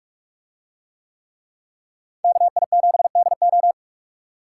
Morse code, heard as a keyed tone, about 45 wpm, 700 Hz. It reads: KI7DO